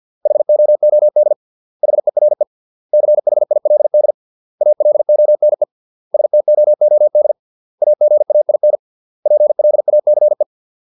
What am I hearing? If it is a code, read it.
Morse code, heard as a keyed tone, about 50 wpm, 600 Hz. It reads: HOOD HERE CHILD ALONE STOOD AGAIN PLACE